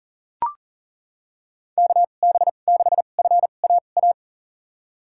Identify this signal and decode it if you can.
Morse code, heard as a keyed tone, about 40 wpm, spaced as for 29 wpm, 700 Hz. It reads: KB6FAA
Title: USA Call Signs - 2x Character Spacing - 40wpm